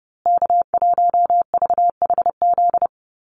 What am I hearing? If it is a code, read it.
Morse code, heard as a keyed tone, about 30 wpm, 700 Hz. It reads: K1VHZ
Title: USA Call Signs Repeated 2x 30wpm